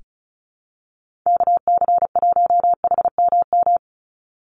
Morse code, heard as a keyed tone, about 34 wpm, 700 Hz. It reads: KC1HMM
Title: USA Call Signs 35wpm